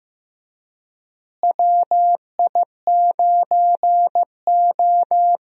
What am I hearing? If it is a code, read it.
Morse code, heard as a keyed tone, about 15 wpm, 700 Hz. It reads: WI9O
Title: USA Call Signs - 4 Characters - Repeated 2x - 15wpm